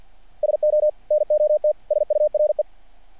Morse code, heard as a keyed tone, about 50 wpm, 600 Hz. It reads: SO NOT SURE
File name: archived / CRSnet